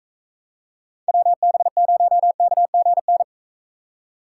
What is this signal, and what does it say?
Morse code, heard as a keyed tone, about 43 wpm, 700 Hz. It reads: WB0KGN